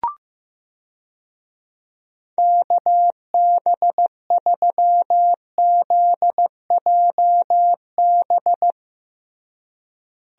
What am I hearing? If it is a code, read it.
Morse code, heard as a keyed tone, about 15 wpm, 700 Hz. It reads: KB3ZJB
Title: USA Call Signs - 6 Characters - Repeated 3x - 15wpm